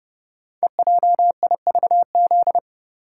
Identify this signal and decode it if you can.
Morse code, heard as a keyed tone, about 30 wpm, 700 Hz. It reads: EJIVZ
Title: Code Groups - 5 Letters - 30wpm